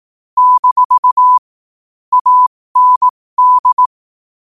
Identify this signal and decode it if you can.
Morse code, heard as a keyed tone, about 18 wpm, spaced as for 15 wpm, 1000 Hz. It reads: - AND
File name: qotdfiles / 15wpm